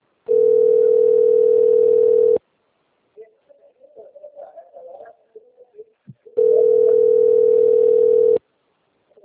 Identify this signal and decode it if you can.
Ringback tone (North America), 440 + 480 Hz.
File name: rg-1600-unknown- - - .8813